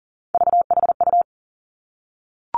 Morse code, heard as a keyed tone, about 40 wpm, 700 Hz. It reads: VHU
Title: Three Challenging Letter-Number Combinations 40wpm